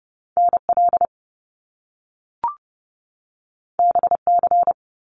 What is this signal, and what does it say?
Morse code, heard as a keyed tone, about 30 wpm, 700 Hz. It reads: NL BC
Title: Canadian Provinces and Territories 30wpm